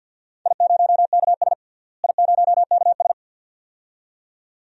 Morse code, heard as a keyed tone, about 48 wpm, 700 Hz. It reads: I0KS I0KS
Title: DXCC-100 Call Signs - 4 Characters - Repeated 2x - 50wpm